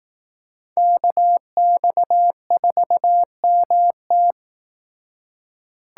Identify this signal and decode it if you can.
Morse code, heard as a keyed tone, about 18 wpm, 700 Hz. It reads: KX4MT
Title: Speed Racing - USA 2x2 Call Signs 15wpm